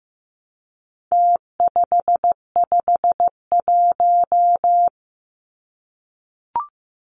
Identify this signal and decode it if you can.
Morse code, heard as a keyed tone, about 15 wpm, 700 Hz. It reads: T551
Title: Numbers - Contest Style - 4-Digit 15wpm